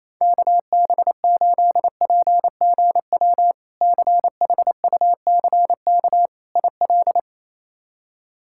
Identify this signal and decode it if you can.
Morse code, heard as a keyed tone, about 28 wpm, 700 Hz. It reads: KB8PGW CHUCK IL
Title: K1USN SST QSO Practice - 28wpm